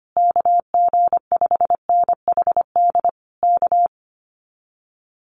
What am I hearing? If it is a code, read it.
Morse code, heard as a keyed tone, about 25 wpm, 700 Hz. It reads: KG5NHD K
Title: Common QSO Phrases Encore 25wpm